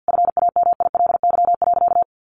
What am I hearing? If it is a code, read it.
Morse code, heard as a keyed tone, about 50 wpm, 700 Hz. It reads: FAMILY.
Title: Sentences from Top 1500 Words 50wpm